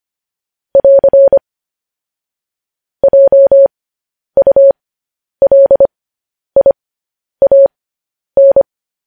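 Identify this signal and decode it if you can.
Morse code, heard as a keyed tone, about 25 wpm, spaced as for 10 wpm, 550 Hz. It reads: <AR> JULIAN
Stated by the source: Morse news headlines practice at 10.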